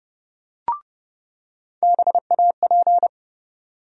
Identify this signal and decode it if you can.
Morse code, heard as a keyed tone, about 30 wpm, 700 Hz. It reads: BAP
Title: USA Call Sign Postfixes - 3 Letters 30wpm